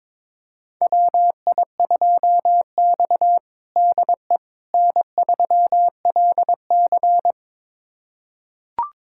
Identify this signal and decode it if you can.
Morse code, heard as a keyed tone, about 22 wpm, 700 Hz. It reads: WI2X DE N3LC
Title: USA Callsign DE Callsign - 4 Characters - 22wpm